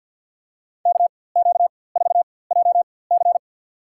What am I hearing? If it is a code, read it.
Morse code, heard as a keyed tone, about 51 wpm, spaced as for 23 wpm, 700 Hz. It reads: KQ4JC
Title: USA Call Signs - 5 Characters - 4x Character Spacing - 50wpm